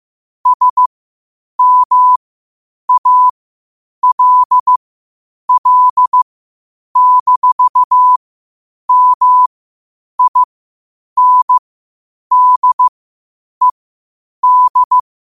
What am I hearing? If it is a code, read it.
Morse code, heard as a keyed tone, about 15 wpm, spaced as for 8 wpm, 1000 Hz. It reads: SMALL-MINDED
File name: qotdfiles / 8wpm